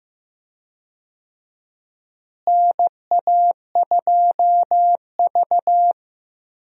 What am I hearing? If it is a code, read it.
Morse code, heard as a keyed tone, about 15 wpm, 700 Hz. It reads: NA2V
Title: USA Call Signs - 4 Characters - Repeated 3x - 15wpm